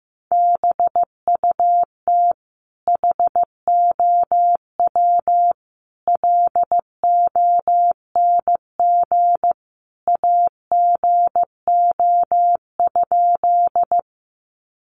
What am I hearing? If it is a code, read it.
Morse code, heard as a keyed tone, about 15 wpm, 700 Hz. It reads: BUT HOW LONG AGO?
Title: Sentences from Top 500 Words 15wpm